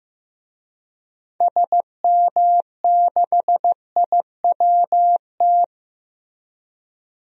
Morse code, heard as a keyed tone, about 15 wpm, 700 Hz. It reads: SM6IWT
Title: DXCC-100 Call Signs - 6 Characters - Repeated 2x - 15wpm